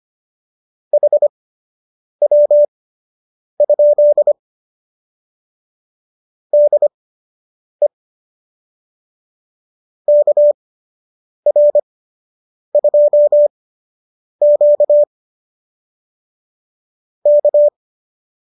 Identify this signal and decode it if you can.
Morse code, heard as a keyed tone, about 25 wpm, spaced as for 8 wpm, 590 Hz. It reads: HW? DE KR2Q K